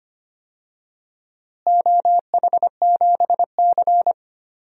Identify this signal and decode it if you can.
Morse code, heard as a keyed tone, about 25 wpm, 700 Hz. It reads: OH7C
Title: DXCC-100 Call Signs - 4 Characters - Repeated 3x - 25wpm